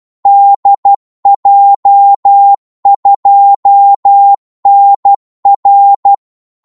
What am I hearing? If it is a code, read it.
Morse code, heard as a keyed tone, about 12 wpm, 800 Hz. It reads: DJ2NR
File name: Morse / qso-12wpm